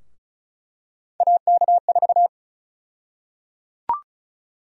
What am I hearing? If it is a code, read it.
Morse code, heard as a keyed tone, about 34 wpm, 700 Hz. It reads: AK4
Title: USA Call Sign Prefixes - 35wpm